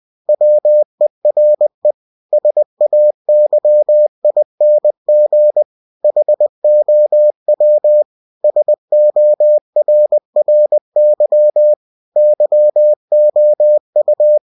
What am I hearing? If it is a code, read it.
Morse code, heard as a keyed tone, about 20 wpm, 600 Hz. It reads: WERE SAYING HOW SORRY YOU
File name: cw-books / winnie-the-pooh / winnie-20wpm-page-0089